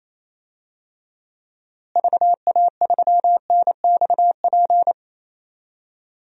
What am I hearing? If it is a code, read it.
Morse code, heard as a keyed tone, about 28 wpm, 700 Hz. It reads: VA3NXP
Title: DXCC-100 Call Signs - 6 Characters - Repeated 2x - 28wpm